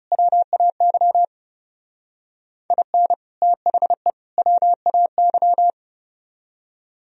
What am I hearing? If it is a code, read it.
Morse code, heard as a keyed tone, about 38 wpm, 700 Hz. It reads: WAY IN THE WAY
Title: Speed-Racing - Sets of 3 Words 30wpm